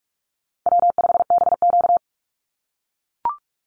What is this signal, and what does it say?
Morse code, heard as a keyed tone, about 44 wpm, 700 Hz. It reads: W5BQ
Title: Easy Call Signs - 45wpm